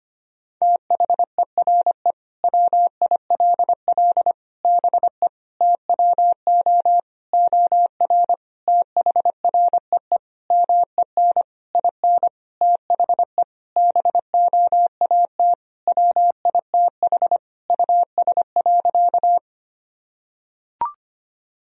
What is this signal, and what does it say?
Morse code, heard as a keyed tone, about 25 wpm, 700 Hz. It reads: THERE WILL BE TWO OR THREE MEN IN THE BOAT WITH US.
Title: Sentences from Top 1000 Words - Review Part 3 - 25wpm